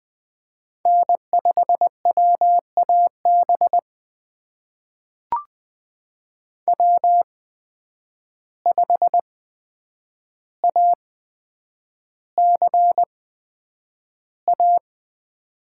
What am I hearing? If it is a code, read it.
Morse code, heard as a keyed tone, about 20 wpm, spaced as for 9 wpm, 700 Hz. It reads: N5WAB W5ACA
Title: USA Call Signs - 5 Characters - 8x Character Spacing - 20wpm